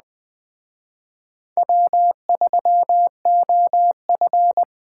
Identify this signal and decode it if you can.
Morse code, heard as a keyed tone, about 20 wpm, 700 Hz. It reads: W3OF